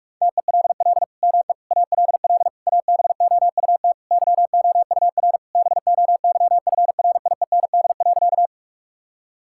Morse code, heard as a keyed tone, about 46 wpm, 700 Hz. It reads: TELL ME ALL ABOUT YOUR BOYFRIEND.
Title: Sentences from Top 1000 Words - Review Part 5 - 45wpm